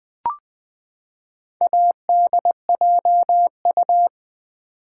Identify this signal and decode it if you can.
Morse code, heard as a keyed tone, about 20 wpm, 700 Hz. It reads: ADJU
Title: Build A Word - 4 Letters At A Time - 10-Letter Words - 20wpm